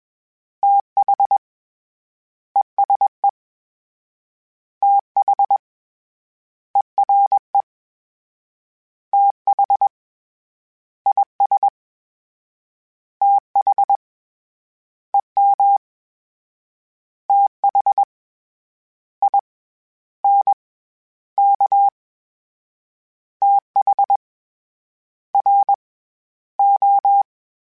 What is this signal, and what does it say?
Morse code, heard as a keyed tone, about 21 wpm, spaced as for 11 wpm, 800 Hz. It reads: THESE THERE THIS THEM THINK THRO